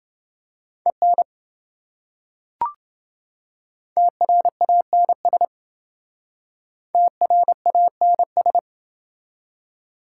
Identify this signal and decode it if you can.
Morse code, heard as a keyed tone, about 28 wpm, 700 Hz. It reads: EN TRANS TRANS